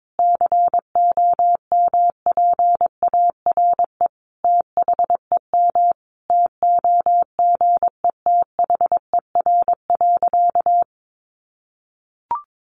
Morse code, heard as a keyed tone, about 22 wpm, 700 Hz. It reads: COMPARE THEM TOGETHER.
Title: Sentences from Top 1300 Words 22wpm